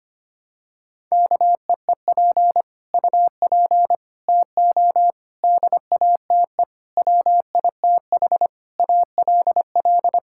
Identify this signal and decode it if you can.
Morse code, heard as a keyed tone, about 25 wpm, 700 Hz. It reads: KEEP UP TO DATE WITH ALL
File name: Sets of 9 Words - 25wpm